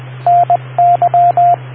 Morse code, heard as a keyed tone, about 20 wpm, 700 Hz, with radio noise. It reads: NY